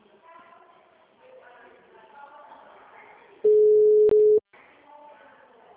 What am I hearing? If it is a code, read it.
Ringback tone (Europe, 425 Hz), with speech.